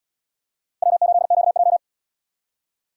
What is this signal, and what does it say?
Morse code, heard as a keyed tone, about 77 wpm, 700 Hz. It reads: W8QJ